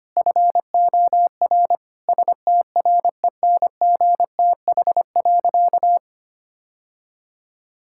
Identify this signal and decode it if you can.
Morse code, heard as a keyed tone, about 25 wpm, 700 Hz. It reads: FOR STRENGTH.
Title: Sentences from Top 1600 Words 25wpm